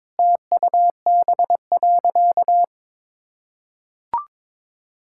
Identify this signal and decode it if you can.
Morse code, heard as a keyed tone, about 22 wpm, 700 Hz. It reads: TUB.
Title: Pangram Challenge v2 - 22wpm